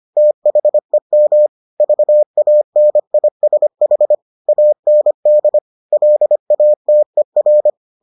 Morse code, heard as a keyed tone, about 25 wpm, 600 Hz. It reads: THEM VANISH AND LATER